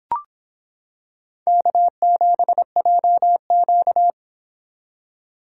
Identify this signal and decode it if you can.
Morse code, heard as a keyed tone, about 26 wpm, 700 Hz. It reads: K7JQ